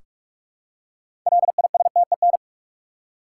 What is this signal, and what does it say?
Morse code, heard as a keyed tone, about 43 wpm, 700 Hz. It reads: LISTEN
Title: Six Letter Words - 45wpm